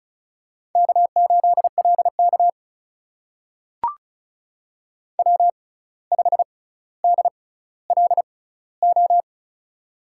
Morse code, heard as a keyed tone, about 36 wpm, spaced as for 17 wpm, 700 Hz. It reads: K8LK W5DLO
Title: USA Call Signs - 6x Character Spacing - 35wpm